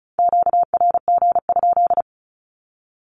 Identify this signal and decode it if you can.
Morse code, heard as a keyed tone, about 36 wpm, 700 Hz. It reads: QRG?
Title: Q-Codes with Spoken Meaning - 35wpm